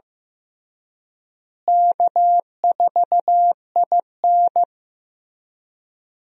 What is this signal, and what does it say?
Morse code, heard as a keyed tone, about 15 wpm, 700 Hz. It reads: K4IN
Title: USA Call Signs - 4 Characters - Repeated 3x - 15wpm